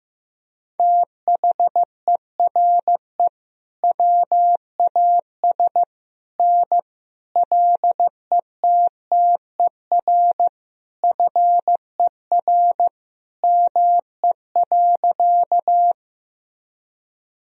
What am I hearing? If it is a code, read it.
Morse code, heard as a keyed tone, about 15 wpm, 700 Hz. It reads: THERE WAS N LETTER FER ME.
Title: Sentences from Top 300 Words and Abbreviations 15wpm